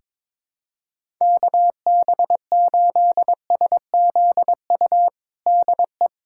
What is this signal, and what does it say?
Morse code, heard as a keyed tone, about 21 wpm, 700 Hz. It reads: KB8SZU DE
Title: USA Callsign DE Callsign - 6 Characters - 22wpm